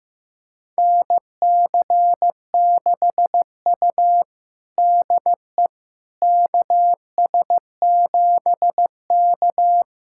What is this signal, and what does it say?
Morse code, heard as a keyed tone, about 15 wpm, 700 Hz. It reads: NC6U DE KS7K
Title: USA Callsign DE Callsign - 4 Characters - 15wpm